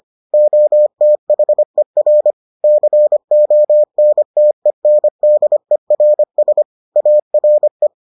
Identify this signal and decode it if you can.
Morse code, heard as a keyed tone, about 25 wpm, 600 Hz. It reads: OTHER CONTENDERS ARE